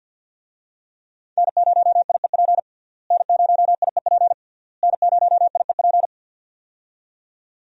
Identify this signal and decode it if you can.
Morse code, heard as a keyed tone, about 52 wpm, 690 Hz. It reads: N0IEP N0IEP N0IEP
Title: USA Call Signs - 5 Characters - Repeated 3x - 50wpm